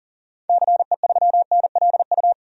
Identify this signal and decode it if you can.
Morse code, heard as a keyed tone, about 40 wpm, 700 Hz. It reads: CE3NLU
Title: DXCC-100 Call Signs - 6 Characters - 40wpm